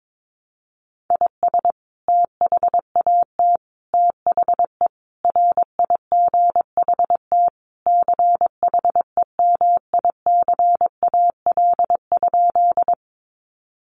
Morse code, heard as a keyed tone, about 22 wpm, 700 Hz. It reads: IS THAT THE RIGHT CHEMICAL?
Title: Sentences from Top 1200 Words 22wpm